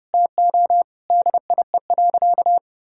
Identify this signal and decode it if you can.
Morse code, heard as a keyed tone, about 32 wpm, 700 Hz. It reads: TO DIE.
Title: Sentences from Top 1600 Words 30wpm